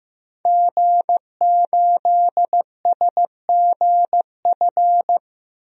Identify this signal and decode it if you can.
Morse code, heard as a keyed tone, about 15 wpm, 700 Hz. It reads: G8SGF
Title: DXCC-100 Call Signs - 5 Characters - 15wpm